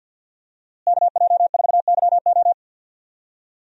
Morse code, heard as a keyed tone, about 48 wpm, 700 Hz. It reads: KJ4YO
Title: USA Call Signs - 5 Characters - Repeated 2x - 50wpm